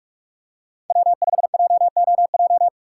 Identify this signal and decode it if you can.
Morse code, heard as a keyed tone, about 40 wpm, 700 Hz. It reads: W5JOJ